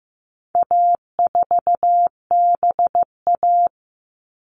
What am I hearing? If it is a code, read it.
Morse code, heard as a keyed tone, about 15 wpm, 700 Hz. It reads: A4BA
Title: All Cut Numbers - 4-Digit 15wpm